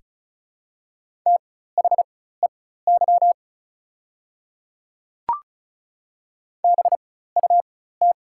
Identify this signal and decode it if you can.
Morse code, heard as a keyed tone, about 36 wpm, spaced as for 16 wpm, 700 Hz. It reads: THEY BUT
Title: 4x Character Spacing - Top 100 Words - 35wpm